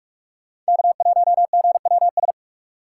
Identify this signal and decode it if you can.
Morse code, heard as a keyed tone, about 59 wpm, 700 Hz. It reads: K1GWS